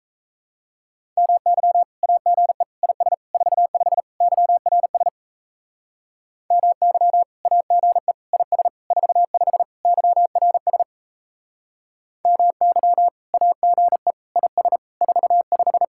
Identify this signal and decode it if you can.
Morse code, heard as a keyed tone, about 36 wpm, 700 Hz. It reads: MY AGE IS 45 YRS MY AGE IS 45 YRS MY AGE IS 45